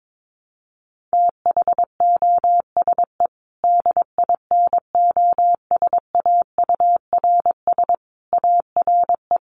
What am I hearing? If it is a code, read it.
Morse code, heard as a keyed tone, about 22 wpm, 700 Hz. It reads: THOSE DINOSAURS ARE
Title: A Sound Of Thunder 22wpm